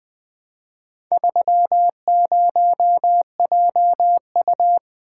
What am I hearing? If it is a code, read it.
Morse code, heard as a keyed tone, about 20 wpm, 700 Hz. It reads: 30JU
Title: Four Letter-Number Combinations - 20wpm